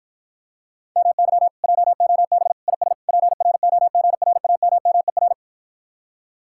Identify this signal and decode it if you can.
Morse code, heard as a keyed tone, about 60 wpm, 700 Hz. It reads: MY JOB IS PROGRAMMER